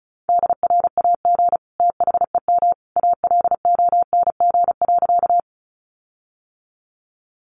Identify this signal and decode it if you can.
Morse code, heard as a keyed tone, about 34 wpm, 700 Hz. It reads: DRAG THEM ALONG.
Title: Sentences from Top 1500 Words 35wpm